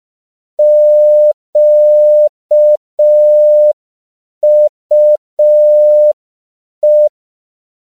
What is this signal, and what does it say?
Morse code, heard as a keyed tone, about 5 wpm, 600 Hz. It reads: QUE